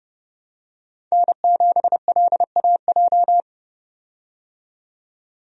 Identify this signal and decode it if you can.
Morse code, heard as a keyed tone, about 30 wpm, 700 Hz. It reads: N7LAJ
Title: USA Call Signs - 5 Characters - 30wpm